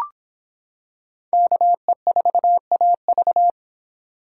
Morse code, heard as a keyed tone, about 25 wpm, 700 Hz. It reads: KE4AV